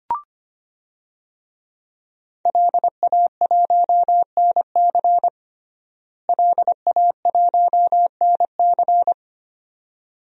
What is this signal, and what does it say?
Morse code, heard as a keyed tone, about 25 wpm, 700 Hz. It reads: LA1NC LA1NC
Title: DXCC-100 Call Signs - 5 Characters - Repeated 2x - 25wpm